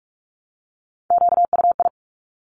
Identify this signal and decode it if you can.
Morse code, heard as a keyed tone, about 46 wpm, 700 Hz. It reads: QUI
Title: Top Three Letter Combinations in Words 45wpm